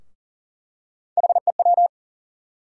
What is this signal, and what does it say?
Morse code, heard as a keyed tone, about 40 wpm, 700 Hz. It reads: HEW
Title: USA Call Sign Postfixes - 3 Letters 40wpm